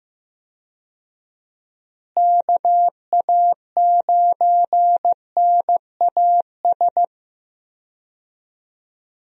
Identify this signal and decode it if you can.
Morse code, heard as a keyed tone, about 15 wpm, 700 Hz. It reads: KA9NAS
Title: USA Call Signs - 6 Characters - Repeated 2x - 15wpm